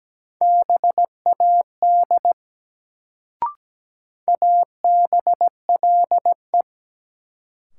Morse code, heard as a keyed tone, about 17 wpm, 700 Hz. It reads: BAD ABLE
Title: Sentences from Top 700 Words - New Word Drill - 17wpm